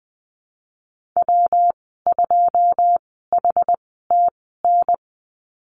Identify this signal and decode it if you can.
Morse code, heard as a keyed tone, about 20 wpm, spaced as for 14 wpm, 700 Hz. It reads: W2HTN
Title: 122 Lesson 20g - Number 3 - Call Signs 20wpm